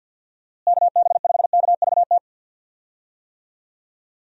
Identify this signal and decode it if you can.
Morse code, heard as a keyed tone, about 51 wpm, 700 Hz. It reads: KB5KVT
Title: USA Call Signs - 6 Characters - Repeated 3x - 50wpm